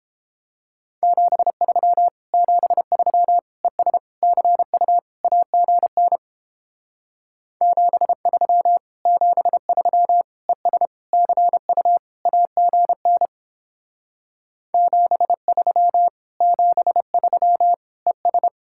Morse code, heard as a keyed tone, about 29 wpm, 700 Hz. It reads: 73 73 ES CU AGN 73 73 ES CU AGN 73 73 ES